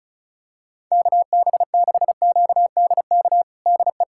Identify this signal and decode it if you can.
Morse code, heard as a keyed tone, about 34 wpm, 700 Hz. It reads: KB6QDK DE
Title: USA Callsign DE Callsign 35wpm